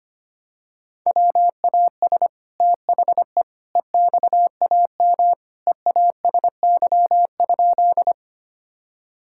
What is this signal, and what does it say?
Morse code, heard as a keyed tone, about 25 wpm, 700 Hz. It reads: WAS THE EXAM EASY?
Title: Sentences from Top 1000 Words - Review Part 5 - 25wpm